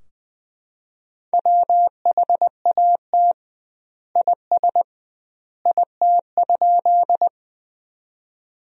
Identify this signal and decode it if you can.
Morse code, heard as a keyed tone, about 21 wpm, 700 Hz. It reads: WHAT IS IT?
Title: 2x Word-Spacing - Sentences from Top 100 Words - 20wpm